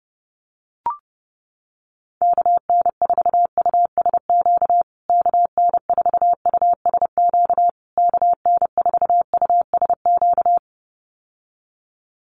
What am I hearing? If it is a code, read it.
Morse code, heard as a keyed tone, about 29 wpm, 700 Hz. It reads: KN4USQ KN4USQ KN4USQ
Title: USA Call Signs Repeated 3x 30wpm